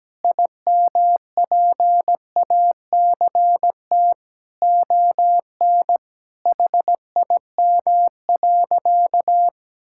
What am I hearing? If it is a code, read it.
Morse code, heard as a keyed tone, about 17 wpm, 700 Hz. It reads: IMPACT ON HIM.
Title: Sentences from Top 1000 Words - Review Part 6 - 17wpm